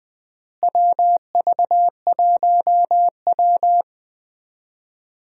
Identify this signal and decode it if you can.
Morse code, heard as a keyed tone, about 20 wpm, 700 Hz. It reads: WV1W
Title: USA Call Signs - 4 Characters - Repeated 3x - 20wpm